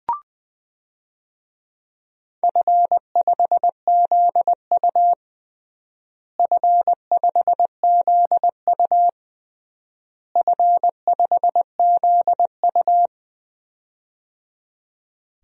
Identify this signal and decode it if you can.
Morse code, heard as a keyed tone, about 21 wpm, 700 Hz. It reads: F5ZU F5ZU F5ZU
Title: DXCC-100 Call Signs - 4 Characters - Repeated 3x - 20wpm